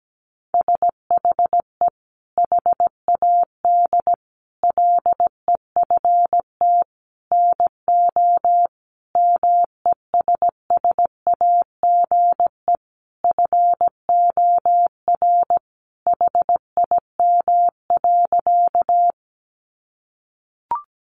Morse code, heard as a keyed tone, about 17 wpm, 700 Hz. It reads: SHE HAD LEFT NO MESSAGE FOR HIM.
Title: Sentences from Top 1100 Words 17wpm